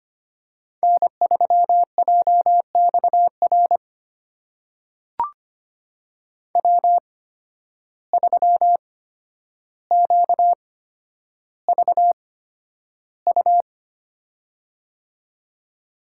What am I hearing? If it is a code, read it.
Morse code, heard as a keyed tone, about 25 wpm, spaced as for 11 wpm, 700 Hz. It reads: N3JXR W3QVU
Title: USA Call Signs - 5 Characters - 8x Character Spacing - 25wpm